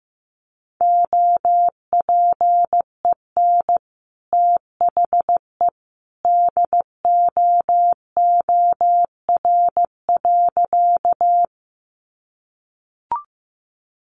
Morse code, heard as a keyed tone, about 15 wpm, 700 Hz. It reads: OPEN THE DOOR.
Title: Sentences from Top 300 Words 15wpm